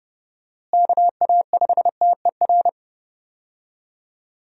Morse code, heard as a keyed tone, about 30 wpm, 700 Hz. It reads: KA5TER
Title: USA Call Signs - 6 Characters - Repeated 3x - 30wpm